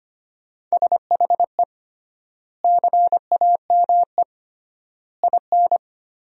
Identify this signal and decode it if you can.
Morse code, heard as a keyed tone, about 25 wpm, 700 Hz. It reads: SHE CAME IN
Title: 3x Word-Spacing - Sentences from Top 300 Words - 25wpm